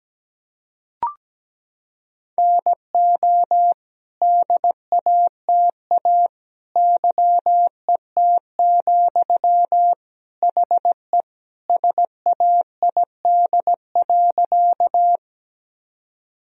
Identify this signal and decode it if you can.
Morse code, heard as a keyed tone, about 17 wpm, 700 Hz. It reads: NO DATA YET, HE SAID.
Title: Sentences from Top 1000 Words - Review Part 3 - 17wpm